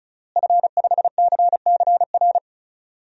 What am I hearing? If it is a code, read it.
Morse code, heard as a keyed tone, about 34 wpm, 700 Hz. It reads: F5CCR
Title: DXCC-100 Call Signs - 5 Characters - Repeated 3x - 35wpm